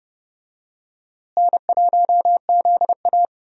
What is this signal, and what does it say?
Morse code, heard as a keyed tone, about 30 wpm, 700 Hz. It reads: N1ZA
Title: USA Call Signs - 4 Characters - Repeated 2x - 30wpm